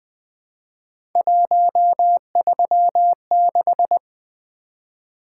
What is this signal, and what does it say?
Morse code, heard as a keyed tone, about 20 wpm, 700 Hz. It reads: 136
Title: Three Letter-Number Combinations - 20wpm